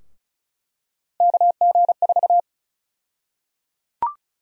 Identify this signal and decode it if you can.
Morse code, heard as a keyed tone, about 36 wpm, 700 Hz. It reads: KG4
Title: USA Call Sign Prefixes - 35wpm